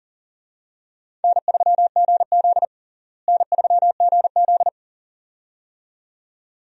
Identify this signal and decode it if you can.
Morse code, heard as a keyed tone, about 40 wpm, 700 Hz. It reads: N3GZ N3GZ
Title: USA Call Signs - 4 Characters - Repeated 2x - 40wpm